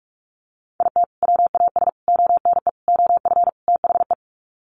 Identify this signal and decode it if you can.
Morse code, heard as a keyed tone, about 44 wpm, 700 Hz. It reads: IT WAS ONE OF THE
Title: Sets of 5 Words 45wpm